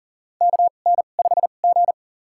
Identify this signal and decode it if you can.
Morse code, heard as a keyed tone, about 40 wpm, spaced as for 29 wpm, 700 Hz. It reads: KN5G
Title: USA Call Signs - 4 Characters - 2x Character Spacing - 40wpm